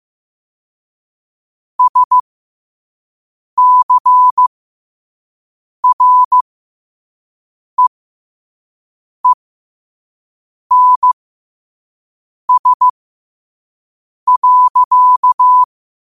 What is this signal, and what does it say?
Morse code, heard as a keyed tone, about 15 wpm, spaced as for 5 wpm, 1000 Hz. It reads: SCREENS.